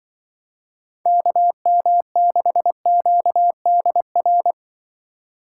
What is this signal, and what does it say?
Morse code, heard as a keyed tone, about 23 wpm, 700 Hz. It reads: KM6QDR
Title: Speed Racing - USA 2x3 Call Signs 20wpm